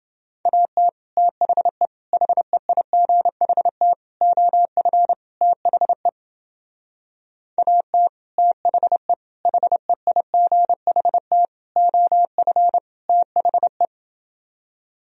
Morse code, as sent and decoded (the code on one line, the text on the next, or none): .- - / - .... / .... .. --. .... - / --- ..-. / - .... / .- - / - .... / .... .. --. .... - / --- ..-. / - .... .
AT THE HEIGHT OF THE AT THE HEIGHT OF THE